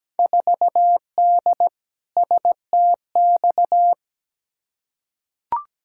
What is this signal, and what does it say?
Morse code, heard as a keyed tone, about 17 wpm, 700 Hz. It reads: 4D STX
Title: Field Day Exchanges Encore - 17wpm